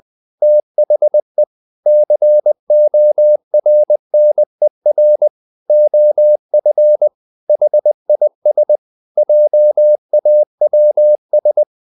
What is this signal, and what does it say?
Morse code, heard as a keyed tone, about 20 wpm, 600 Hz. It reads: THE CORNER OF HIS JAWS
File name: the old man and the sea part 2 20wpm